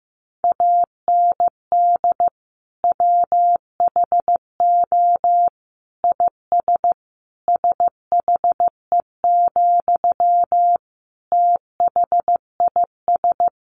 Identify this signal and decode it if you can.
Morse code, heard as a keyed tone, about 15 wpm, 700 Hz. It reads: AND WHO IS SHE, THIS
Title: Sentences from Top 1100 Words 15wpm